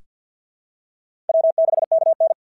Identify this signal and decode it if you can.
Morse code, heard as a keyed tone, about 46 wpm, 660 Hz. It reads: W6KN
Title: SOTA QSO Practice 50wpm